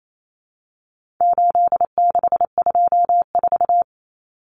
Morse code, heard as a keyed tone, about 28 wpm, 700 Hz. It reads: 8624
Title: Common Cut Numbers - 4-Digit 28wpm